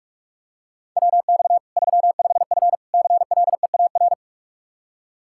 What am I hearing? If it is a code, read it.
Morse code, heard as a keyed tone, about 41 wpm, 700 Hz. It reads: WX 35F CLEAR
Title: K1USN SST QSO Practice 45wpm